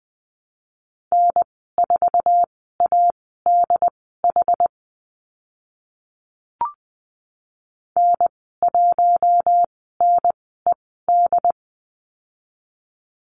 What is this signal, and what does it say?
Morse code, heard as a keyed tone, about 20 wpm, spaced as for 14 wpm, 700 Hz. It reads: N4ADH N1NED
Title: 069 Lesson 12g - Letter d - Call Signs 20wpm